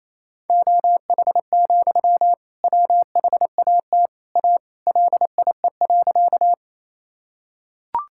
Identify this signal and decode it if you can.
Morse code, heard as a keyed tone, about 28 wpm, 700 Hz. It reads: OH, WHAT A LIE.
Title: Sentences from Top 1600 Words 28wpm